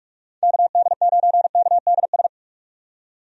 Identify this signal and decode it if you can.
Morse code, heard as a keyed tone, about 46 wpm, 700 Hz. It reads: KD9KDS
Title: USA Call Signs - 6 Characters - 45wpm